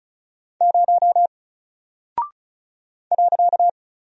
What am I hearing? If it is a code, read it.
Morse code, heard as a keyed tone, about 36 wpm, 700 Hz. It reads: 0 .
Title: Single Character (Letter:Number:Punctuation:Prosign) Alt 35wpm